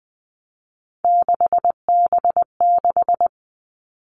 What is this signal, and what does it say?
Morse code, heard as a keyed tone, about 20 wpm, 700 Hz. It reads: 6B6
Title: All Cut Numbers - 3-Digit 20wpm